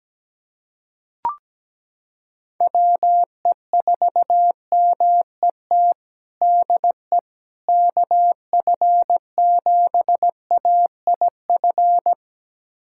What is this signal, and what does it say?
Morse code, heard as a keyed tone, about 17 wpm, 700 Hz. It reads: WE4MET DE KF7AIF
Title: USA Callsign DE Callsign - 6 Characters - 17wpm